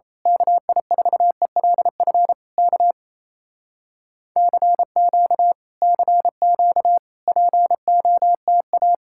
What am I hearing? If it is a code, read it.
Morse code, heard as a keyed tone, about 26 wpm, 700 Hz. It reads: KI4ELF K CQ CQ POTA
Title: Speed-Racing Common QSO Phrases - 28wpm